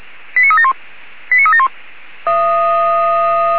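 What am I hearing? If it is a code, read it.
Five-tone selective call (ZVEI), heard as quick runs of five tones, tones of about 70 ms. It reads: 89371 89371
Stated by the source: ZVEI five-tone call received by SDR.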